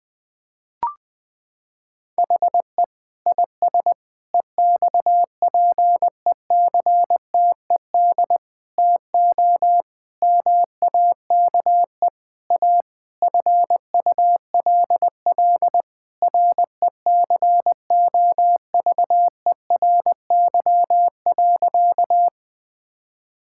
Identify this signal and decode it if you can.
Morse code, heard as a keyed tone, about 20 wpm, 700 Hz. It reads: HE IS EXPECTED TO MAKE A FULL RECOVERY.
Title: Sentences with 7-Word Phrases - 20wpm